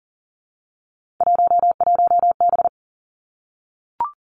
Morse code, heard as a keyed tone, about 42 wpm, 700 Hz. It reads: 11B
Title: Three Challenging Letter-Number Combinations 40wpm